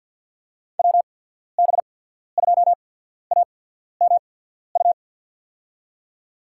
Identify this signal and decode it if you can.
Morse code, heard as a keyed tone, about 53 wpm, spaced as for 14 wpm, 700 Hz. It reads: WB2AMU
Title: USA Call Signs - 6 Characters - 8x Character Spacing - 50wpm